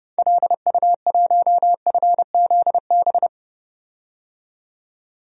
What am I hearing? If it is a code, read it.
Morse code, heard as a keyed tone, about 32 wpm, 700 Hz. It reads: LU1FZB
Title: DXCC-100 Call Signs - 6 Characters - 30wpm